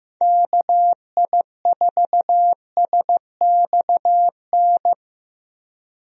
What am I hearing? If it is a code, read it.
Morse code, heard as a keyed tone, about 15 wpm, 700 Hz. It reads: KI4SXN
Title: USA Call Signs - 6 Characters - Repeated 3x - 15wpm